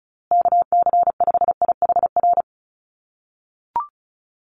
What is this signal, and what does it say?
Morse code, heard as a keyed tone, about 34 wpm, 700 Hz. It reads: KC5IHR
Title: USA Call Signs Repeated 3x 35wpm